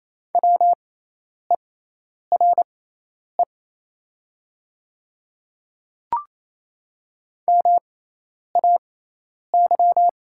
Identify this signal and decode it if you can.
Morse code, heard as a keyed tone, about 28 wpm, spaced as for 10 wpm, 700 Hz. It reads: WERE MAY
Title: 6x Character Spacing - Top 500 Words - 28wpm